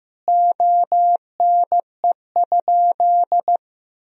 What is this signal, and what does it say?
Morse code, heard as a keyed tone, about 16 wpm, 700 Hz. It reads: ONE?
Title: Sentences from Top 1000 Words - Review Part 4 - 15wpm